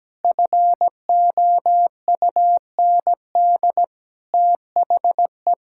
Morse code, heard as a keyed tone, about 17 wpm, 700 Hz. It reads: FOUND THE
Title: Sentences from Top 1000 Words - Review Part 6 - 17wpm